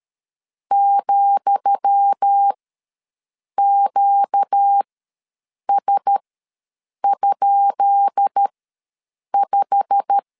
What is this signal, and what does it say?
Morse code, heard as a keyed tone, about 13 wpm, spaced as for 7 wpm, 800 Hz. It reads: ,QS?5
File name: LEZIONI / 10 SIGNS / 10D COMA ws 12wpm 3618